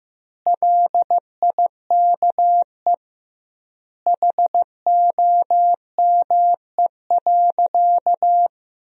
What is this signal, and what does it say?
Morse code, heard as a keyed tone, about 15 wpm, 700 Hz. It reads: LIKE HOME.
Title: 2x Word-Spacing - Sentences from Top 200 Words - 15wpm